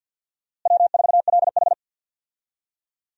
Morse code, heard as a keyed tone, about 51 wpm, 700 Hz. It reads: W4LH
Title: USA Call Signs - 4 Characters - Repeated 2x - 50wpm